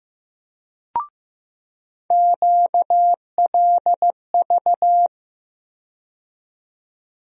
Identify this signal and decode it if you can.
Morse code, heard as a keyed tone, about 15 wpm, 700 Hz. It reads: QLV
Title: USA Call Sign Postfixes - 3 Letters 15wpm